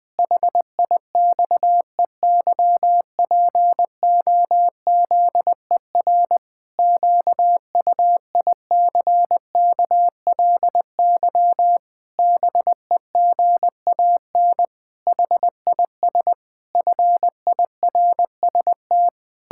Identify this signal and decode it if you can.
Morse code, heard as a keyed tone, about 20 wpm, 700 Hz. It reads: HIXEYPOZER QUICKLY BEGAN HIS FIRST
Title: Pangram Challenge v2 - 20wpm